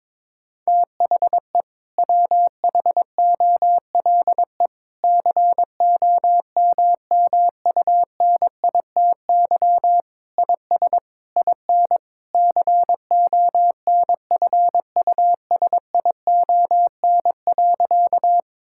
Morse code, heard as a keyed tone, about 22 wpm, 700 Hz. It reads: THE WHOLE COMMUNITY IS IN CONFUSION.
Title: Sentences from Top 1600 Words 22wpm